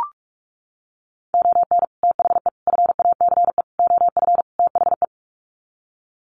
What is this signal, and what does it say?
Morse code, heard as a keyed tone, about 44 wpm, 700 Hz. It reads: ON THE FACE OF THE
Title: Sets of 5 Words Encore2 45wpm